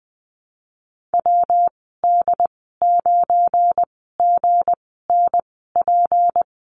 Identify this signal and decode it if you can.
Morse code, heard as a keyed tone, about 20 wpm, spaced as for 14 wpm, 700 Hz. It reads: WD9GNP
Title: 173 Lesson 28g - Number 9 - Call Signs 20wpm